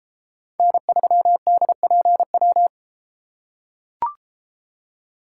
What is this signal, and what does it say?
Morse code, heard as a keyed tone, about 32 wpm, 700 Hz. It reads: N3DPW